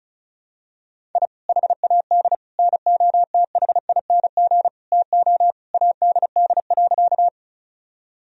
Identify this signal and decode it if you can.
Morse code, heard as a keyed tone, about 36 wpm, 700 Hz. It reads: I HAD NOTHING TO ADD.
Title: Sentences from Top 1000 Words - Review Part 1 - 35wpm